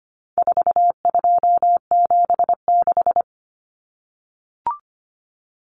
Morse code, heard as a keyed tone, about 25 wpm, 700 Hz. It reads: 4276
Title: Common Cut Numbers - 4-Digit 25wpm